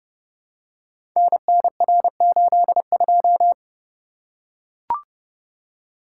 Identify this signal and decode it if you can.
Morse code, heard as a keyed tone, about 29 wpm, 700 Hz. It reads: NNR82
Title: Code Groups - 5 Letter:Numbers - 30wpm